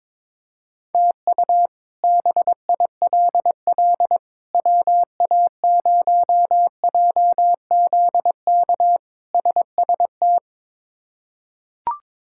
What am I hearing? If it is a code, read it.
Morse code, heard as a keyed tone, about 22 wpm, 700 Hz. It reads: TU BILL WA0JZK SST
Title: K1USN SST QSO Practice - 22wpm